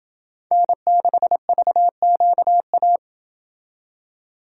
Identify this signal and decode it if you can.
Morse code, heard as a keyed tone, about 28 wpm, 700 Hz. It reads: N6VQA